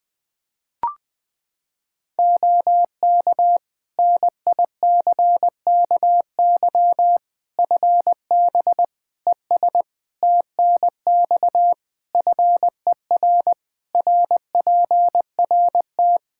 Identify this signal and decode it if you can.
Morse code, heard as a keyed tone, about 20 wpm, 700 Hz. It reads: OK NICKY FB ES TNX FER RPRT
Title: Practice QSO - Exchange 2 - Info - 20wpm